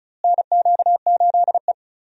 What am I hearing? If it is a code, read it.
Morse code, heard as a keyed tone, about 36 wpm, 700 Hz. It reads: NQ8E
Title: USA Call Signs - 4 Characters - Repeated 2x - 35wpm